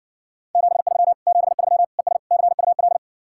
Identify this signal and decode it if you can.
Morse code, heard as a keyed tone, about 60 wpm, 700 Hz. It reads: 73 73 ES CUL